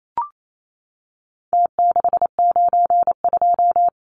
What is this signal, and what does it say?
Morse code, heard as a keyed tone, about 28 wpm, 700 Hz. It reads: T692
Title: Numbers - Contest Style - 4-Digit 28wpm